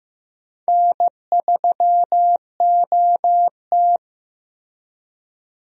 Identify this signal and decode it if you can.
Morse code, heard as a keyed tone, about 15 wpm, 700 Hz. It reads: N3OT
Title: USA Call Signs - 4 Characters - Repeated 3x - 15wpm